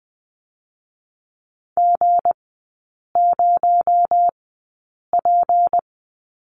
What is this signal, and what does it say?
Morse code, heard as a keyed tone, about 20 wpm, spaced as for 8 wpm, 700 Hz. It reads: G0P
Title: 218 Lesson 35e - Number 0 - Three Characters 20wpm